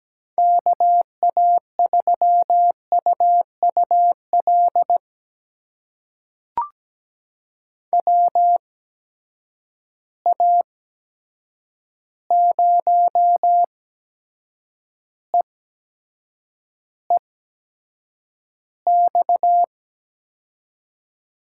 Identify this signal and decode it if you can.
Morse code, heard as a keyed tone, about 17 wpm, spaced as for 7 wpm, 700 Hz. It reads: KA3UUL WA0EEX
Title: USA Call Signs - 6 Characters - 8x Character Spacing - 17wpm